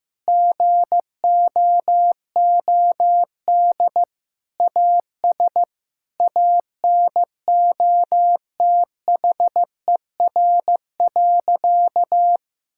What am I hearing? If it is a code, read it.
Morse code, heard as a keyed tone, about 16 wpm, 700 Hz. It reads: GOOD AS ANOTHER.
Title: Sentences from Top 1600 Words 15wpm